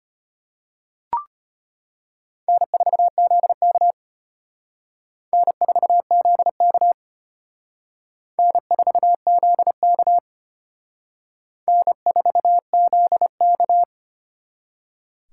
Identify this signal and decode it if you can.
Morse code, heard as a keyed tone, about 33 wpm, 700 Hz. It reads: N4ZK N4ZK N4ZK N4ZK